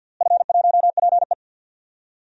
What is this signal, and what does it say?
Morse code, heard as a keyed tone, about 47 wpm, 700 Hz. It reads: F1PE
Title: DXCC-100 Call Signs - 4 Characters - Repeated 3x - 50wpm